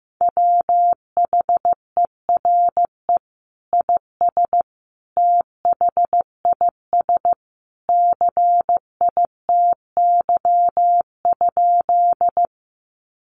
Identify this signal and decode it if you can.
Morse code, heard as a keyed tone, about 15 wpm, 700 Hz. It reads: WHERE IS THIS CITY?
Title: Sentences from Top 300 Words 15wpm